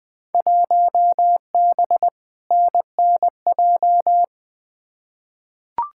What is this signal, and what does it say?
Morse code, heard as a keyed tone, about 20 wpm, 700 Hz. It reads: 1B NNJ
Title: Field Day Exchanges Encore - 20wpm